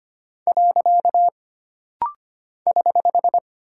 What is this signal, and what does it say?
Morse code, heard as a keyed tone, about 25 wpm, 700 Hz. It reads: . <HH>
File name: Single Punctuation-Prosign - Rapid-Fire - 25wpm